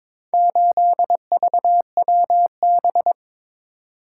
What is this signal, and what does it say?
Morse code, heard as a keyed tone, about 22 wpm, 700 Hz. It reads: 8VWB
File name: practice / 22wpm / Four Letter-Number Combinations - 22wpm